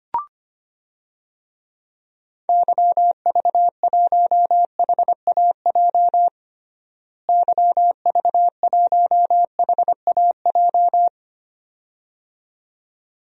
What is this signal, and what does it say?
Morse code, heard as a keyed tone, about 25 wpm, 700 Hz. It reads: YV1HAJ YV1HAJ
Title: DXCC-100 Call Signs - 6 Characters - Repeated 2x - 25wpm